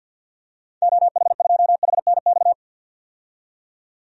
Morse code, heard as a keyed tone, about 50 wpm, 700 Hz. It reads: OH2HNX
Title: DXCC-100 Call Signs - 6 Characters - 50wpm